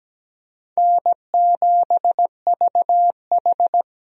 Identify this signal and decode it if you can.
Morse code, heard as a keyed tone, about 17 wpm, 700 Hz. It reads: N7VH
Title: USA Call Signs - 4 Characters - 17wpm